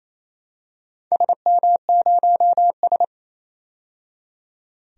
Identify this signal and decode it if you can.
Morse code, heard as a keyed tone, about 28 wpm, 700 Hz. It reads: SM0S
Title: DXCC-100 Call Signs - 4 Characters - 28wpm